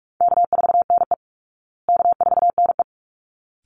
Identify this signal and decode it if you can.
Morse code, heard as a keyed tone, about 44 wpm, 700 Hz. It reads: K4NE K4NE
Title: Easy Call Signs Repeated 2x - 45wpm